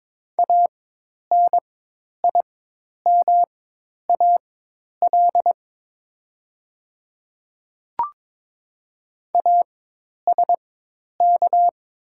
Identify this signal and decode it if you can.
Morse code, heard as a keyed tone, about 22 wpm, spaced as for 10 wpm, 700 Hz. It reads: ANIMAL ASK
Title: 4x Character Spacing - Top 500 Words - 22wpm